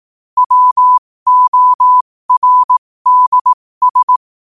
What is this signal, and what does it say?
Morse code, heard as a keyed tone, about 18 wpm, spaced as for 15 wpm, 1000 Hz. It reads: WORDS
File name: qotdfiles / 15wpm